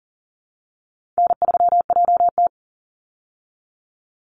Morse code, heard as a keyed tone, about 40 wpm, 700 Hz. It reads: N3JT
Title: Easy Call Signs Repeated 2x - 40wpm